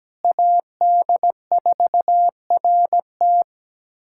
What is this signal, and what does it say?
Morse code, heard as a keyed tone, about 17 wpm, 700 Hz. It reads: AD4RT
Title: USA Callsign DE Callsign 17wpm